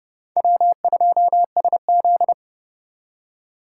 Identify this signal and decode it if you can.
Morse code, heard as a keyed tone, about 35 wpm, 700 Hz. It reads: W2SZ
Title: Speed Racing - USA 1x2 Call Signs 30wpm